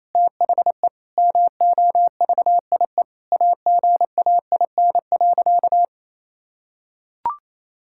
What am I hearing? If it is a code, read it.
Morse code, heard as a keyed tone, about 28 wpm, 700 Hz. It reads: THE MOVIE AGAIN.
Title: Sentences from Top 1000 Words - Review Part 5 - 28wpm